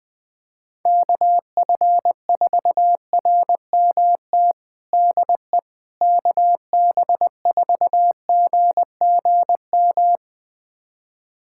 Morse code, heard as a keyed tone, about 20 wpm, 700 Hz. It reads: KF4RMT DE KB4GGM
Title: USA Callsign DE Callsign - 6 Characters - 20wpm